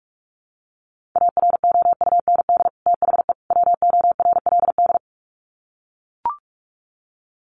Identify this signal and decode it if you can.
Morse code, heard as a keyed tone, about 44 wpm, 700 Hz. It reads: AROUND THE WORLD
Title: Sets of 3 Words 45wpm